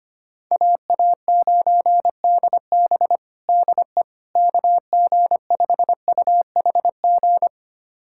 Morse code, heard as a keyed tone, about 25 wpm, 700 Hz. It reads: AA9DB DE KG5UHG
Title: USA Callsign DE Callsign 25wpm